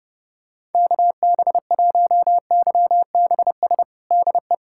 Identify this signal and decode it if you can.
Morse code, heard as a keyed tone, about 30 wpm, 700 Hz. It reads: KB1YBS DE
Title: USA Callsign DE Callsign - 6 Characters - 30wpm